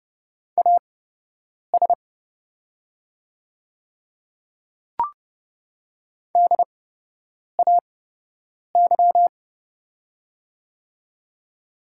Morse code, heard as a keyed tone, about 31 wpm, spaced as for 8 wpm, 700 Hz. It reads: AS DAY
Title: 8x Character Spacing - Top 100 Words - 30wpm